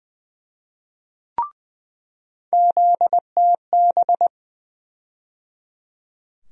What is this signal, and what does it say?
Morse code, heard as a keyed tone, about 20 wpm, 700 Hz. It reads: ZTB
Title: USA Call Sign Postfixes - 3 Letters 20wpm